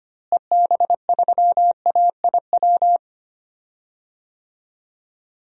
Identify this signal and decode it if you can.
Morse code, heard as a keyed tone, about 25 wpm, 700 Hz. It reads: EB3AIW
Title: DXCC-100 Call Signs - 6 Characters - Repeated 3x - 25wpm